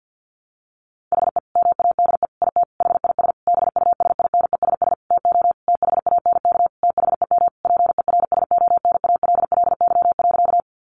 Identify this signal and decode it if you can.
Morse code, heard as a keyed tone, about 50 wpm, 700 Hz. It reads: HE MADE IT HIS BUSINESS TO THANK THEM PERSONALLY.
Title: Sentences from Top 1400 Words 50wpm